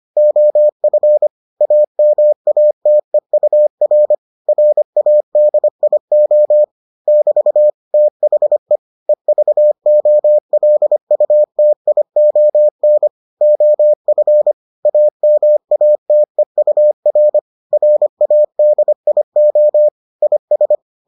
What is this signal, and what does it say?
Morse code, heard as a keyed tone, about 25 wpm, 600 Hz. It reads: OF AMATEUR RADIO <BT> THE EVOLUTION OF AMATEUR RADIO IS